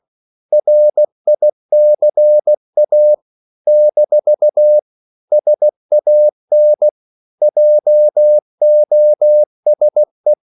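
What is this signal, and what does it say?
Morse code, heard as a keyed tone, about 16 wpm, 600 Hz. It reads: RICA - SAN JOSE
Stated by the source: CW practice of national capitals at 16 words per minute, from ON6ZQ.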